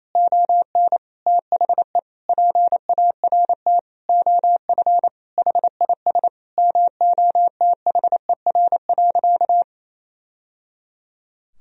Morse code, heard as a keyed tone, about 28 wpm, 700 Hz. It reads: ON THE PART OF HIS MOTHER.
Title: Sentences from Top 1000 Words - Review Part 4 - 28wpm